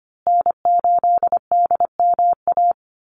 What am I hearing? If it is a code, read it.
Morse code, heard as a keyed tone, about 25 wpm, 700 Hz. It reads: N8DMA
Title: USA Call Signs Repeated 3x 25wpm